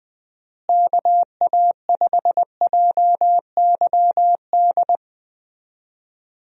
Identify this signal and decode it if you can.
Morse code, heard as a keyed tone, about 20 wpm, 700 Hz. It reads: KA5JYD
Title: USA Call Signs - 6 Characters - Repeated 2x - 20wpm